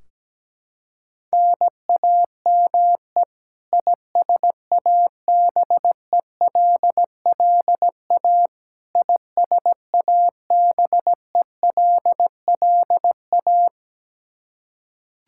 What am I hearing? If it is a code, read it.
Morse code, heard as a keyed tone, about 17 wpm, 700 Hz. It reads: NAME ISABELLA ISABELLA
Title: Practice QSO - Exchange 1 - Beginning - 17wpm